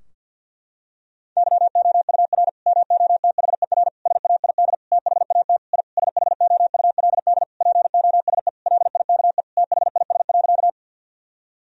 Morse code, heard as a keyed tone, about 46 wpm, 700 Hz. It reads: YOUR MOTHER SAID THAT I SHOULD POSE LIKE THIS.
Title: Sentences from Top 1000 Words - Review Part 2 - 50wpm